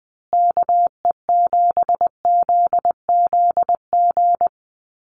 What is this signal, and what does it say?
Morse code, heard as a keyed tone, about 20 wpm, 700 Hz. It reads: KE7ZZG
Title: USA Call Signs Repeated 2x 20wpm